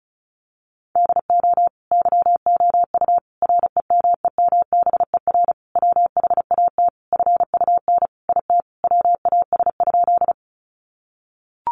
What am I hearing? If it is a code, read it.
Morse code, heard as a keyed tone, about 34 wpm, 700 Hz. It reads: DO YOU REMEMBER WHAT FUN IT WAS?
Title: Sentences from Top 700 Words 35wpm